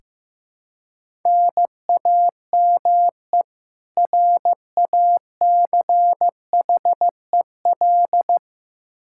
Morse code, heard as a keyed tone, about 15 wpm, 700 Hz. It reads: NAME RACHEL
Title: Practice QSO - Exchange 1 - Beginning - 15wpm